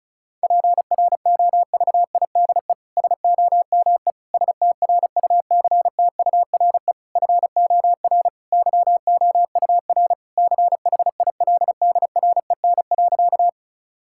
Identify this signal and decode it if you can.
Morse code, heard as a keyed tone, about 34 wpm, 700 Hz. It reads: PROVIDE SOME STRUCTURE FOR YOUR CHILDREN.
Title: Sentences from Top 1000 Words - Review Part 6 - 35wpm